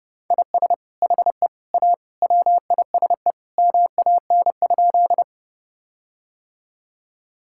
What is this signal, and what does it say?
Morse code, heard as a keyed tone, about 32 wpm, 700 Hz. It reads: IS HE A WISE MAN?
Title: Sentences from Top 1600 Words 30wpm